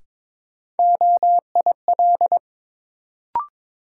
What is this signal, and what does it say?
Morse code, heard as a keyed tone, about 22 wpm, 700 Hz. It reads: OIL
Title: Sentences from Top 1000 Words - Review Part 4 - New Word Drill - 22wpm